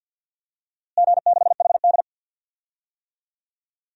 Morse code, heard as a keyed tone, about 46 wpm, 700 Hz. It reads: G6HD
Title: DXCC-100 Call Signs - 4 Characters - Repeated 2x - 50wpm